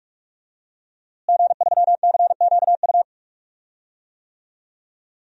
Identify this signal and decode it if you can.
Morse code, heard as a keyed tone, about 46 wpm, 700 Hz. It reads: G3CQU
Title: DXCC-100 Call Signs - 5 Characters - Repeated 2x - 45wpm